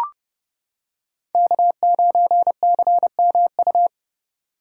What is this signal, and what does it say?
Morse code, heard as a keyed tone, about 30 wpm, 700 Hz. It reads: K9CMU